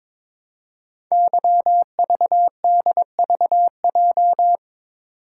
Morse code, heard as a keyed tone, about 23 wpm, 700 Hz. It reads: YVDVJ
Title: Code Groups - 5 Letters - 22wpm